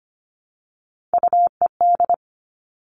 Morse code, heard as a keyed tone, about 25 wpm, 700 Hz. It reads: UED
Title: Three Challenging Letter-Number Combinations 25wpm